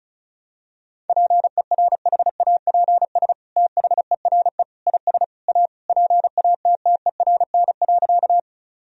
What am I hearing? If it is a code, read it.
Morse code, heard as a keyed tone, about 36 wpm, 700 Hz. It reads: PERHAPS THERE IS A PATTERN.
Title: Sentences from Top 1000 Words - Review Part 2 - 35wpm